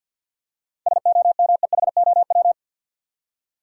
Morse code, heard as a keyed tone, about 51 wpm, 700 Hz. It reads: SOMEHOW
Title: Seven Letter Words - 50wpm